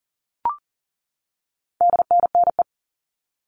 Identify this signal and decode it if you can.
Morse code, heard as a keyed tone, about 40 wpm, 700 Hz. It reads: DNNE
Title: All Cut Numbers - 4-Digit 40wpm